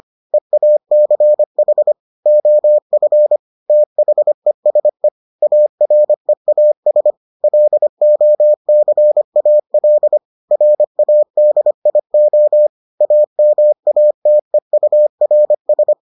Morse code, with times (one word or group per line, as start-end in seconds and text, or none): 0.3-1.9 EACH
2.3-3.4 OF
3.7-5.1 THESE
5.4-7.1 AREAS
7.4-10.2 LOCAL
10.5-12.7 RADIO
13.0-15.9 AMATEURS